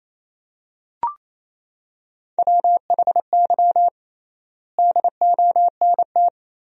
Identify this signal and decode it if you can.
Morse code, heard as a keyed tone, about 28 wpm, 700 Hz. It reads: WHY DONT
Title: 3x Word-Spacing - Sentences from Top 300 Words - 28wpm